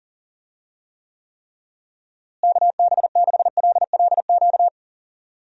Morse code, heard as a keyed tone, about 37 wpm, 700 Hz. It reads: KB6LLQ
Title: USA Call Signs - 6 Characters - Repeated 3x - 40wpm